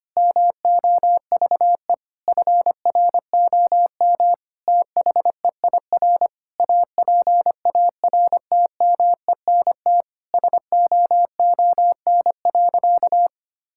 Morse code, heard as a keyed tone, about 25 wpm, 700 Hz. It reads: MOVE FROM THEIR APARTMENT SOON.
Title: Sentences from Top 1000 Words - Review Part 6 - 25wpm